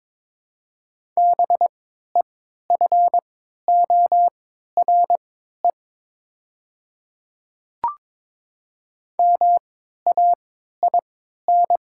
Morse code, heard as a keyed tone, about 22 wpm, spaced as for 12 wpm, 700 Hz. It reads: BEFORE MAIN
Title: 3x Character Spacing - Top 300 Words - 22wpm